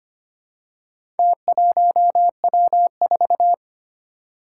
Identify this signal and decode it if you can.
Morse code, heard as a keyed tone, about 25 wpm, 700 Hz. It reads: T1W4
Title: Four Challenging Letter-Number Combinations 25wpm